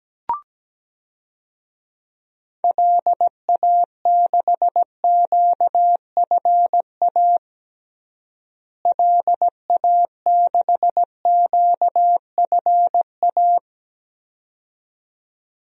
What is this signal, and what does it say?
Morse code, heard as a keyed tone, about 17 wpm, 700 Hz. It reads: LA6QFA LA6QFA
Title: DXCC-100 Call Signs - 6 Characters - Repeated 2x - 17wpm